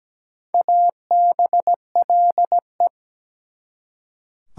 Morse code, heard as a keyed tone, about 17 wpm, 700 Hz. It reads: ABLE
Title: Prefixes and Suffixes 17wpm